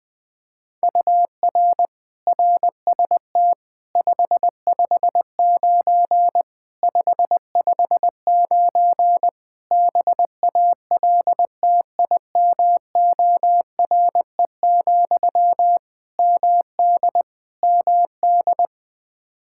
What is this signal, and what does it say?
Morse code, heard as a keyed tone, about 21 wpm, 700 Hz. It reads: UR RST 559 559 BALTIMORE, MD MD
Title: Practice QSO - Exchange 1 - Beginning - 20wpm